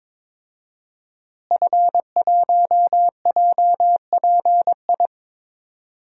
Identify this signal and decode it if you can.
Morse code, heard as a keyed tone, about 22 wpm, 700 Hz. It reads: F1JPI
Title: DXCC-100 Call Signs - 5 Characters - Repeated 2x - 22wpm